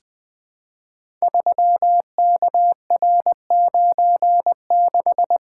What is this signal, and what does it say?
Morse code, heard as a keyed tone, about 20 wpm, 700 Hz. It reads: 3KR96
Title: Code Groups - 5 Letter:Numbers - 20wpm